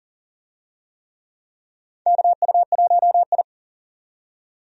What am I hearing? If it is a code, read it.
Morse code, heard as a keyed tone, about 40 wpm, 700 Hz. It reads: KU1I
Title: USA Call Signs - 4 Characters - 40wpm